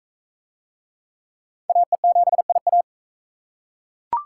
Morse code, heard as a keyed tone, about 39 wpm, 700 Hz. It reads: AE7IA